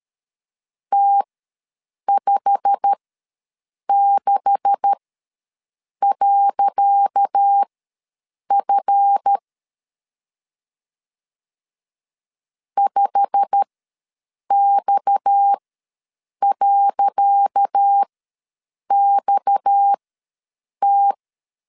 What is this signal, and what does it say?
Morse code, heard as a keyed tone, about 13 wpm, spaced as for 7 wpm, 800 Hz. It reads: T56.F 5X.XT
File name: LEZIONI / 10 SIGNS / 10C DOT ws 12wpm 3548